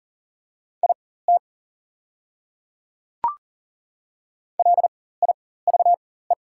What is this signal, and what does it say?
Morse code, heard as a keyed tone, about 40 wpm, spaced as for 19 wpm, 700 Hz. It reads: IT LIVE
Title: 4x Character Spacing - Top 300 Words - 40wpm